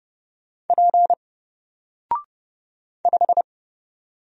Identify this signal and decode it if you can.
Morse code, heard as a keyed tone, about 30 wpm, 700 Hz. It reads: P 5
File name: Single-Letter-Number-30wpm